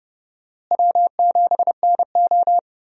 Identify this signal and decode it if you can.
Morse code, heard as a keyed tone, about 29 wpm, 700 Hz. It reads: W7NO
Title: USA Call Signs - 4 Characters - Repeated 2x - 30wpm